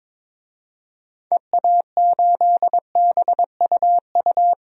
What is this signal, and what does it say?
Morse code, heard as a keyed tone, about 22 wpm, 700 Hz. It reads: EA8BUU
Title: DXCC-100 Call Signs - 6 Characters - 22wpm